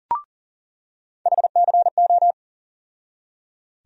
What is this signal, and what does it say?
Morse code, heard as a keyed tone, about 38 wpm, 700 Hz. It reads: HCO
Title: USA Call Sign Postfixes - 3 Letters 40wpm